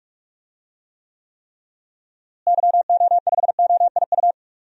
Morse code, heard as a keyed tone, about 43 wpm, 700 Hz. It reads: YO5OIU
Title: DXCC-100 Call Signs - 6 Characters - Repeated 3x - 45wpm